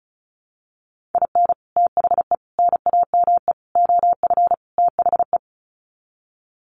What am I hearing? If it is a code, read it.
Morse code, heard as a keyed tone, about 34 wpm, 700 Hz. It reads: IN THE NAME OF THE
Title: Sets of 5 Words Encore 35wpm